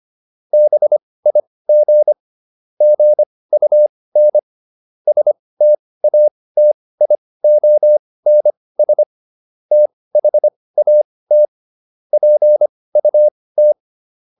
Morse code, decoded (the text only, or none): BIG GUN STATIONS THAT PUT